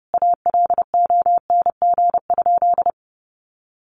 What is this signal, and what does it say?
Morse code, heard as a keyed tone, about 30 wpm, 700 Hz. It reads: ALONG?
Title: Sentences from Top 1300 Words 30wpm